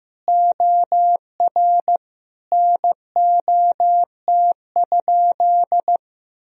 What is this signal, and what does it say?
Morse code, heard as a keyed tone, about 15 wpm, 700 Hz. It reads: OR NOT?
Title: Sentences from Top 1000 Words - Review Part 3 - 15wpm